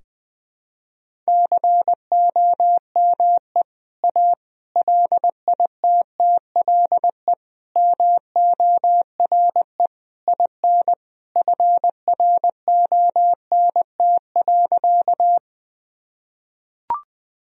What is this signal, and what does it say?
Morse code, heard as a keyed tone, about 20 wpm, 700 Hz. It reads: COME A LITTLE MORE IN FRONT.
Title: Sentences from Top 1000 Words - Review Part 2 - 20wpm